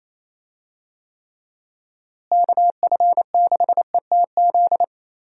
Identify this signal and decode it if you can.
Morse code, heard as a keyed tone, about 28 wpm, 700 Hz. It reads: KF6ETZ
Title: USA Call Signs - 6 Characters - Repeated 3x - 28wpm